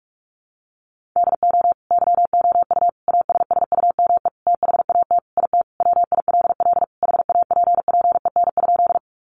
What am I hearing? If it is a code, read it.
Morse code, heard as a keyed tone, about 44 wpm, 700 Hz. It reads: DO YOU ASSUME THAT IT WILL HAPPEN?
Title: Sentences from Top 1500 Words 45wpm